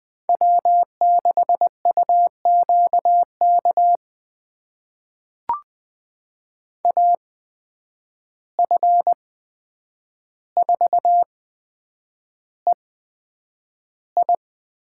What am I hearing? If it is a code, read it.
Morse code, heard as a keyed tone, about 20 wpm, spaced as for 9 wpm, 700 Hz. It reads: W6UQK AF4EI
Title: USA Call Signs - 5 Characters - 8x Character Spacing - 20wpm